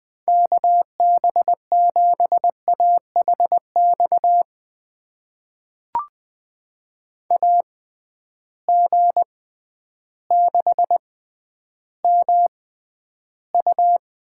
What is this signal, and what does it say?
Morse code, heard as a keyed tone, about 21 wpm, spaced as for 11 wpm, 700 Hz. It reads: KB7AHX AG6MU
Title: USA Call Signs - 6x Character Spacing - 20wpm